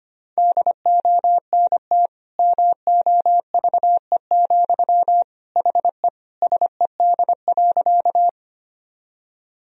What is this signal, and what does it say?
Morse code, heard as a keyed tone, about 25 wpm, 700 Hz. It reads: DONT MOVE, HE SED.
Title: Sentences from Top 300 Words and Abbreviations 25wpm